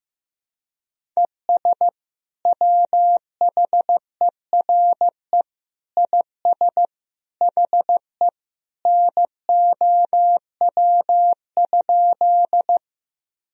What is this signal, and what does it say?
Morse code, heard as a keyed tone, about 16 wpm, 700 Hz. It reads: ES WHERE IS HE NOW?
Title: Sentences from Top 200 Words and Abbreviations 15wpm